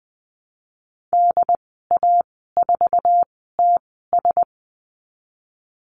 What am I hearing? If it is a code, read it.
Morse code, heard as a keyed tone, about 20 wpm, spaced as for 15 wpm, 700 Hz. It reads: DA4TS
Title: 088 Lesson 15g - Number 5 - Call Signs 20wpm